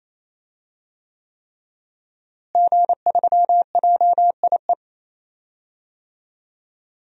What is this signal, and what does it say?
Morse code, heard as a keyed tone, about 28 wpm, 700 Hz. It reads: G3JIE
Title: DXCC-100 Call Signs - 5 Characters - Repeated 3x - 28wpm